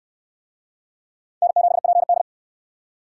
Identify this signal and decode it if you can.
Morse code, heard as a keyed tone, about 64 wpm, 700 Hz. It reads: N7PD